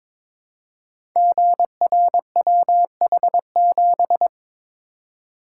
Morse code, heard as a keyed tone, about 22 wpm, 700 Hz. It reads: GRWH7
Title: Code Groups - 5 Letter:Numbers - 22wpm